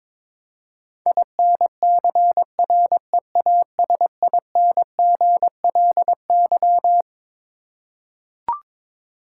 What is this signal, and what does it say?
Morse code, heard as a keyed tone, about 22 wpm, 700 Hz. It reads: INCREASINGLY
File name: Twelve Letter Words - 22wpm